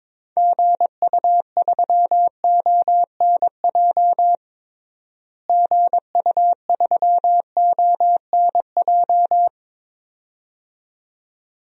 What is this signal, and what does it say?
Morse code, heard as a keyed tone, about 22 wpm, 700 Hz. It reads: GU3ONJ GU3ONJ
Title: DXCC-100 Call Signs - 6 Characters - Repeated 2x - 22wpm